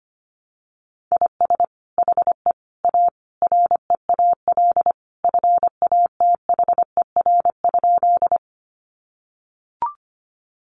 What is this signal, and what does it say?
Morse code, heard as a keyed tone, about 25 wpm, 700 Hz. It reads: IS HE A REAL FATHER?
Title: Sentences from Top 300 Words 25wpm